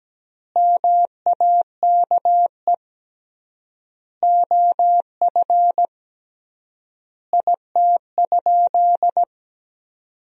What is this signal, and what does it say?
Morse code, heard as a keyed tone, about 17 wpm, 700 Hz. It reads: MAKE OF IT?
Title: 3x Word-Spacing - Sentences from Top 100 Words - 17wpm